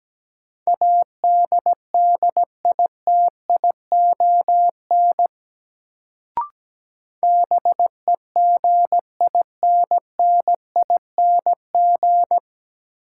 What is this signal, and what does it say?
Morse code, heard as a keyed tone, about 17 wpm, 700 Hz. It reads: ADDITION BEGINNING
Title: Sets of 3 Words - New Word Drill - 17wpm